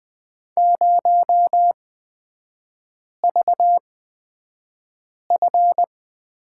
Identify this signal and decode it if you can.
Morse code, heard as a keyed tone, about 20 wpm, spaced as for 5 wpm, 700 Hz. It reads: 0VF
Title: 229 Lesson 37e - Letter x - Three Characters 20wpm